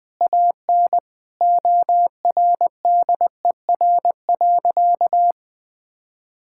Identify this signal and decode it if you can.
Morse code, heard as a keyed tone, about 20 wpm, 700 Hz. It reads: AN ORDER.
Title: Sentences from Top 1600 Words 20wpm